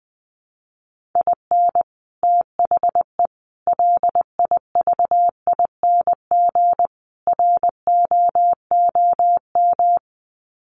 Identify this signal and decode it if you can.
Morse code, heard as a keyed tone, about 20 wpm, 700 Hz. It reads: IN THE LIVING ROOM
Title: Sets of 4 Words 20wpm